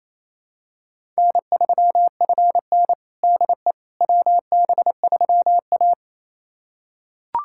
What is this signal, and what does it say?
Morse code, heard as a keyed tone, about 28 wpm, 700 Hz. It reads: N3FN DE WB3A
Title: USA Callsign DE Callsign - 4 Characters - 28wpm